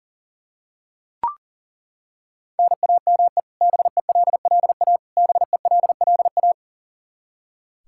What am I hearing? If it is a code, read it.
Morse code, heard as a keyed tone, about 37 wpm, 700 Hz. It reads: NAME BELLA BELLA
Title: Practice QSO - Exchange 1 - Beginning - 40wpm